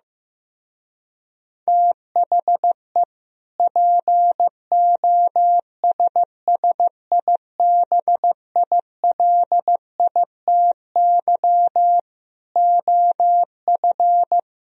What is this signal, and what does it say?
Morse code, heard as a keyed tone, about 15 wpm, 700 Hz. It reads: THE POSSIBILITY OF
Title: Speed-Racing - Sets of 3 Words 15wpm